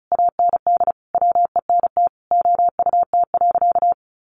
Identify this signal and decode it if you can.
Morse code, heard as a keyed tone, about 34 wpm, 700 Hz. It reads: AND WENT OUT.
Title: Sentences from Top 300 Words 35wpm